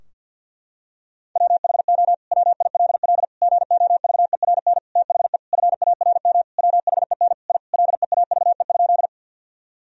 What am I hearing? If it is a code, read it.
Morse code, heard as a keyed tone, about 50 wpm, 700 Hz. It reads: WHO WILL GOVERN THE FARM WHEN I LEAVE?
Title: Sentences from Top 1000 Words - Review Part 2 - 50wpm